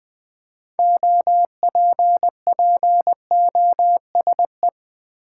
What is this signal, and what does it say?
Morse code, heard as a keyed tone, about 20 wpm, 700 Hz. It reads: OPPOSE
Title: 6 Letter Words - First 3 Letters Given - 20wpm